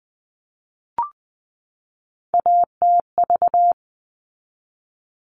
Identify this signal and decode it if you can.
Morse code, heard as a keyed tone, about 20 wpm, 700 Hz. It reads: ATV
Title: All Cut Numbers - 3-Digit 20wpm